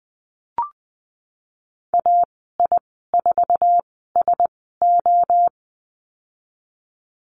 Morse code, heard as a keyed tone, about 20 wpm, spaced as for 15 wpm, 700 Hz. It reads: AI4SO
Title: 048 Lesson 9g - Number 4 - Call Signs 20wpm